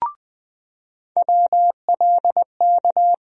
Morse code, heard as a keyed tone, about 20 wpm, 700 Hz. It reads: WLK
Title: USA Call Sign Postfixes - 3 Letters 20wpm